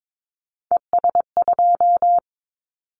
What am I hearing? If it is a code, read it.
Morse code, heard as a keyed tone, about 22 wpm, 700 Hz. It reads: ES2
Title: Callsign Prefixes for Top 100 DXCC - 22wpm